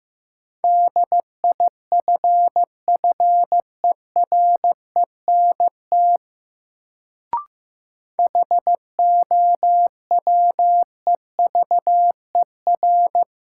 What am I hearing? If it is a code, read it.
Morse code, heard as a keyed tone, about 15 wpm, 700 Hz. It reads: DIFFERENT HOWEVER
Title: Sentences from Top 600 Words - New Word Drill - 15wpm